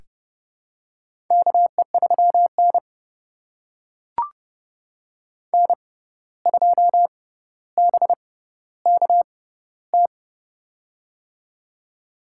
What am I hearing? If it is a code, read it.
Morse code, heard as a keyed tone, about 30 wpm, spaced as for 14 wpm, 700 Hz. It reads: KE3N N2BKT
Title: USA Call Signs - 6x Character Spacing - 30wpm